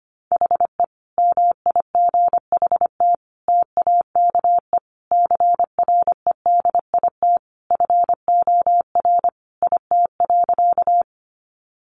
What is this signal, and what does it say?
Morse code, heard as a keyed tone, about 25 wpm, 700 Hz. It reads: HE MIGHT TAKE CREDIT FOR IT.
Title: Sentences from Top 1000 Words 25wpm